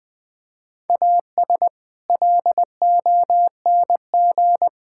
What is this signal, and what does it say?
Morse code, heard as a keyed tone, about 20 wpm, 700 Hz. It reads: AS LONG
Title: Build A Phrase - 4-Word Phrases - 20wpm